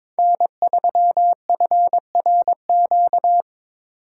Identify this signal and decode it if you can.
Morse code, heard as a keyed tone, about 22 wpm, 700 Hz. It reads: N3FRQ
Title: USA Call Signs - 4x Character Spacing - 22wpm